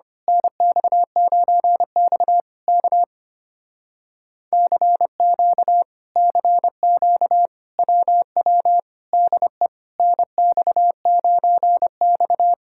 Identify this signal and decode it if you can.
Morse code, heard as a keyed tone, about 24 wpm, 700 Hz. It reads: NX9X K CQ CQ WW DE NX9X
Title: Speed-Racing Common QSO Phrases - 25wpm